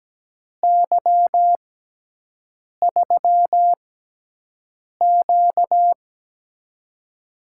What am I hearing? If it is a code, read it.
Morse code, heard as a keyed tone, about 17 wpm, spaced as for 6 wpm, 700 Hz. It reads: Y3Q
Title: USA Call Signs - 4 Characters - 6x Character Spacing - 17wpm